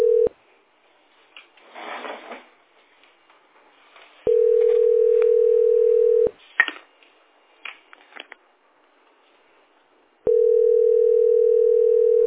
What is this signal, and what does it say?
Ringback tone (North America), 440 + 480 Hz.